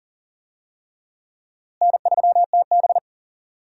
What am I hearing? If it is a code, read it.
Morse code, heard as a keyed tone, about 40 wpm, 700 Hz. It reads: N3TB
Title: USA Call Signs - 4 Characters - Repeated 3x - 40wpm